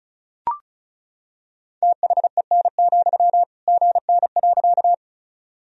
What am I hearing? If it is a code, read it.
Morse code, heard as a keyed tone, about 36 wpm, 700 Hz. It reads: THEN, GN.
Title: Sentences from Top 300 Words and Abbreviations 35wpm